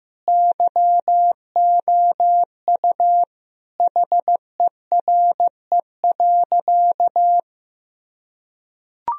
Morse code, heard as a keyed tone, about 15 wpm, 700 Hz. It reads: YOU HERE.
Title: Sentences from Top 1000 Words - Review Part 2 - 15wpm